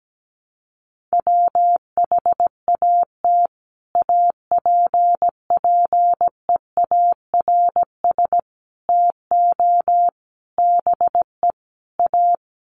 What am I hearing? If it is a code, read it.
Morse code, heard as a keyed tone, about 17 wpm, 700 Hz. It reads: WHAT APPEARS TO BE A
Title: Sets of 5 Words Encore2 17wpm